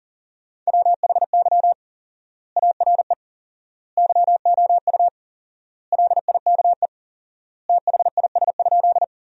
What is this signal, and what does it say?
Morse code, heard as a keyed tone, about 40 wpm, 700 Hz. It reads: WHY ARE YOU LIKE THIS?
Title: 4x Word-Spacing - Sentences from Top 200 Words - 40wpm